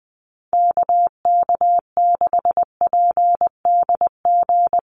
Morse code, heard as a keyed tone, about 20 wpm, 700 Hz. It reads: KK6PDG
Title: USA Call Signs Repeated 3x 20wpm